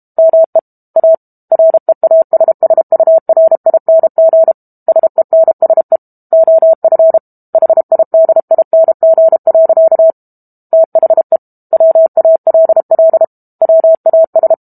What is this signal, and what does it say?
Morse code, heard as a keyed tone, about 32 wpm, 650 Hz. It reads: ME A REASSURING SENSE OF HIDING. THE WALL WAS
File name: Book Two / 06-Chapter 6 - 32 WPM